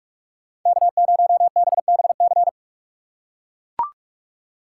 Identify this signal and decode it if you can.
Morse code, heard as a keyed tone, about 42 wpm, 700 Hz. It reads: K0BBC
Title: USA Call Signs - 5 Characters - 4x Character Spacing - 45wpm